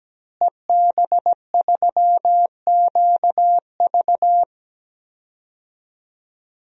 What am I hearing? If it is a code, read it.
Morse code, heard as a keyed tone, about 17 wpm, 700 Hz. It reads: EB3QV
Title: DXCC-100 Call Signs - 5 Characters - Repeated 3x - 17wpm